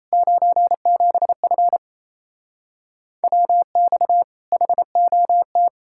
Morse code, heard as a keyed tone, about 37 wpm, 700 Hz. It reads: 97F WX HOT